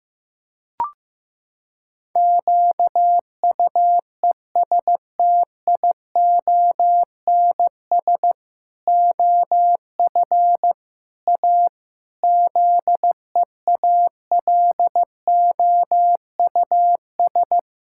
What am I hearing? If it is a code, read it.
Morse code, heard as a keyed tone, about 15 wpm, 700 Hz. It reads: QUESTIONS OF A ZEALOUS
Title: Pangram Challenge v2 - 15wpm